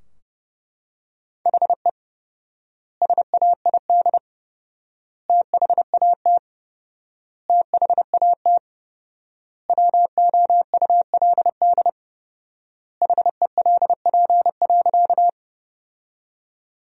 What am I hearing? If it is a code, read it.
Morse code, heard as a keyed tone, about 30 wpm, 700 Hz. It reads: HE SAID THAT THAT WOULD HELP.
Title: 4x Word-Spacing - Sentences from Top 200 Words - 30wpm